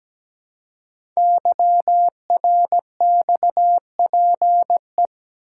Morse code, heard as a keyed tone, about 17 wpm, 700 Hz. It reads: YRXPE
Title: Code Groups - 5 Letters - 17wpm